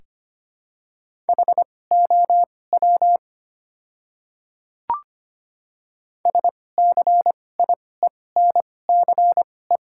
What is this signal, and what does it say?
Morse code, heard as a keyed tone, about 25 wpm, spaced as for 18 wpm, 700 Hz. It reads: HOW SCIENCE
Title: 2x Character Spacing - Top 500 Words - 25wpm